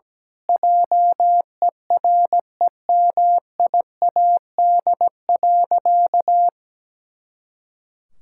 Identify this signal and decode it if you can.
Morse code, heard as a keyed tone, about 17 wpm, 700 Hz. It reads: JEREMIAD.
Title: Pangram Challenge v2 - 17wpm